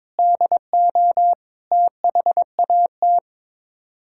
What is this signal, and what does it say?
Morse code, heard as a keyed tone, about 22 wpm, 700 Hz. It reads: DO THAT
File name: practice / 22wpm / Sets of 8 Words - 22wpm